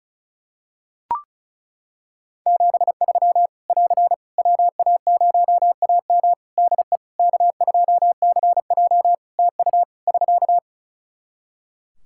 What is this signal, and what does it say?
Morse code, heard as a keyed tone, about 34 wpm, 700 Hz. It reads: 73 <AR> WA0AM DE K2CJ TU <SK>
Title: K1USN SST QSO Practice 35wpm